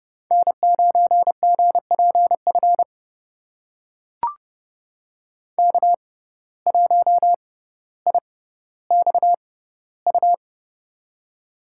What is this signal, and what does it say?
Morse code, heard as a keyed tone, about 30 wpm, spaced as for 15 wpm, 700 Hz. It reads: N9GPF K1IXU
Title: USA Call Signs - 5 Characters - 6x Character Spacing - 30wpm